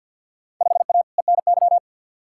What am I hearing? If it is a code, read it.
Morse code, heard as a keyed tone, about 49 wpm, 700 Hz. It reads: 5A ENY
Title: Field Day Exchanges - 50wpm